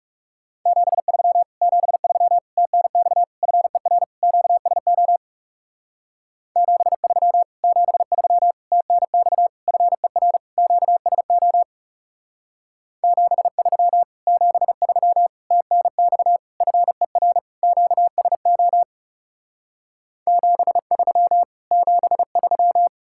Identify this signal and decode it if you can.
Morse code, heard as a keyed tone, about 37 wpm, 700 Hz. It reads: 73 73 TNX FER QSO 73 73 TNX FER QSO 73 73 TNX FER QSO 73 73
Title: Speed-Racing Common QSO Phrases - 30wpm